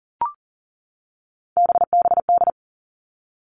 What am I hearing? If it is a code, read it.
Morse code, heard as a keyed tone, about 40 wpm, 700 Hz. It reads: BBD
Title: All Cut Numbers - 3-Digit 40wpm